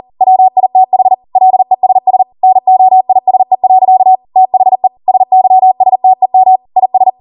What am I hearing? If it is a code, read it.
Morse code, heard as a keyed tone, about 40 wpm, 750 Hz. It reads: WITH LESS NOISE. THE SYSTEM IS
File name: Morse / 40WPM